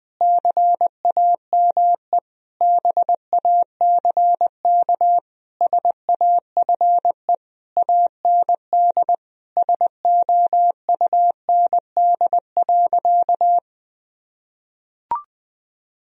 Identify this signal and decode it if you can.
Morse code, heard as a keyed tone, about 20 wpm, 700 Hz. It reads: CAME BACK SAFE AND SOUND.
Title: Sentences from Top 1000 Words - Review Part 1 - 20wpm